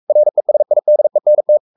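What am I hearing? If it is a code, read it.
Morse code, heard as a keyed tone, about 44 wpm, 600 Hz. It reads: RESIDENT
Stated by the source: Morse code at 43 wpm sending Ken Follett's novel Never in English, from CW med Gnister.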